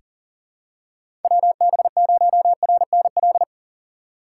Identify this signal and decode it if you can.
Morse code, heard as a keyed tone, about 40 wpm, 700 Hz. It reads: WB0RNL
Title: USA Call Signs - 6 Characters - 6x Character Spacing - 40wpm